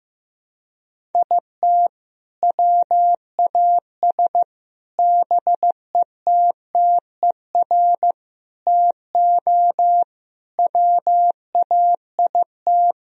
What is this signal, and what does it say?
Morse code, heard as a keyed tone, about 15 wpm, 700 Hz. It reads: IT WAS BETTER TO WAIT
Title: Sentences from Top 1000 Words - Review Part 1 - 15wpm